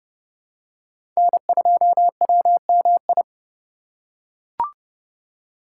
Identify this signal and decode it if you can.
Morse code, heard as a keyed tone, about 30 wpm, 700 Hz. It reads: N2WMI
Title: USA Call Signs - 4x Character Spacing - 30wpm